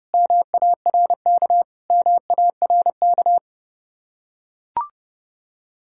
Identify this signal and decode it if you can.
Morse code, heard as a keyed tone, about 30 wpm, 700 Hz. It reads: MARK MARK